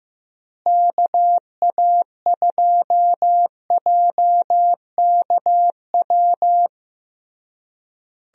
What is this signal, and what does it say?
Morse code, heard as a keyed tone, about 15 wpm, 700 Hz. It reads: KA2JKW
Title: USA Call Signs - 6 Characters - Repeated 3x - 15wpm